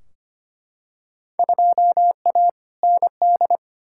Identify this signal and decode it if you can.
Morse code, heard as a keyed tone, about 25 wpm, 700 Hz. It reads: 2A ND
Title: Field Day Exchanges Encore - 25wpm